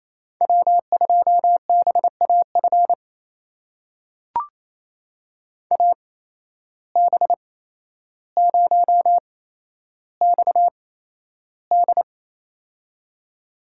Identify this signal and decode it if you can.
Morse code, heard as a keyed tone, about 28 wpm, spaced as for 12 wpm, 700 Hz. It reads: W2BAF AB0XD
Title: USA Call Signs - 5 Characters - 8x Character Spacing - 28wpm